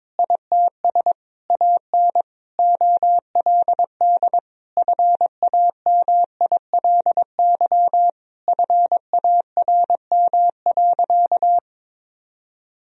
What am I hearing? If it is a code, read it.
Morse code, heard as a keyed tone, about 22 wpm, 700 Hz. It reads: ITS AN OLD FAMILY FARM.
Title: Sentences from Top 1000 Words - Review Part 2 - 22wpm